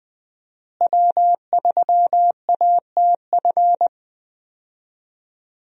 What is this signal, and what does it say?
Morse code, heard as a keyed tone, about 20 wpm, 700 Hz. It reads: W3ATF
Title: USA Call Signs - 5 Characters - Repeated 2x - 20wpm